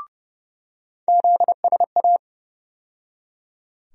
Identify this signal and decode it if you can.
Morse code, heard as a keyed tone, about 32 wpm, 700 Hz. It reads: ZSA
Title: USA Call Sign Postfixes - 3 Letters 30wpm